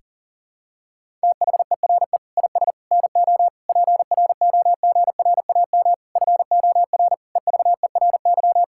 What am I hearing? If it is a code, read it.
Morse code, heard as a keyed tone, about 40 wpm, 700 Hz. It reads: THERE IS NO PROGRAM FOR EVERY
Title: Sentences from Top 1000 Words - Review Part 5 - 40wpm